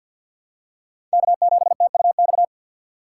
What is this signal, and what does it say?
Morse code, heard as a keyed tone, about 53 wpm, 700 Hz. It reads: K7TUX
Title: USA Call Signs - 5 Characters - 6x Character Spacing - 50wpm